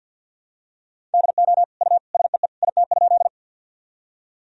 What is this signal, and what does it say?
Morse code, heard as a keyed tone, about 49 wpm, 700 Hz. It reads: DO U SEE IT?
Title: Sentences from Top 100 Words and Abbreviations 50wpm